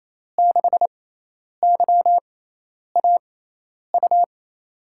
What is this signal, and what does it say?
Morse code, heard as a keyed tone, about 28 wpm, spaced as for 10 wpm, 700 Hz. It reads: 6YAU
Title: USA Call Signs - 6 Characters - 6x Character Spacing - 28wpm